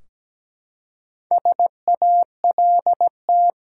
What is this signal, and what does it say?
Morse code, heard as a keyed tone, about 17 wpm, 700 Hz. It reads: SALT
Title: Four Letter Words - 17wpm